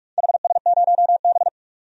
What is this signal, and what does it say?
Morse code, heard as a keyed tone, about 42 wpm, 700 Hz. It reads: HS0B
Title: DXCC-100 Call Signs - 4 Characters - 45wpm